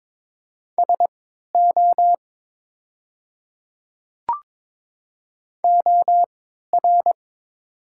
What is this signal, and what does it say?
Morse code, heard as a keyed tone, about 22 wpm, spaced as for 12 wpm, 700 Hz. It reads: SO OR
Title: 3x Character Spacing - Top 100 Words - 22wpm